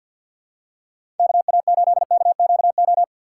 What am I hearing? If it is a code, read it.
Morse code, heard as a keyed tone, about 47 wpm, 700 Hz. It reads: KA8KQO
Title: USA Call Signs - 6 Characters - 4x Character Spacing - 50wpm